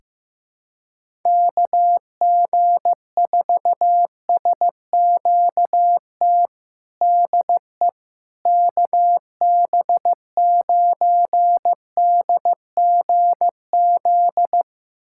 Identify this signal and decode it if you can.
Morse code, heard as a keyed tone, about 15 wpm, 700 Hz. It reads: KG4SQT DE KB9DGZ
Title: USA Callsign DE Callsign - 6 Characters - 15wpm